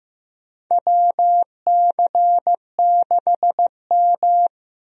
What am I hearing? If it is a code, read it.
Morse code, heard as a keyed tone, about 16 wpm, 700 Hz. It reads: WC6M
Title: USA Call Signs - 4 Characters - Repeated 2x - 15wpm